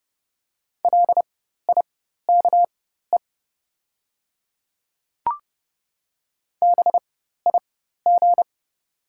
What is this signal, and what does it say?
Morse code, heard as a keyed tone, about 30 wpm, spaced as for 14 wpm, 700 Hz. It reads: LIKE BIG
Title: 4x Character Spacing - Top 500 Words - 30wpm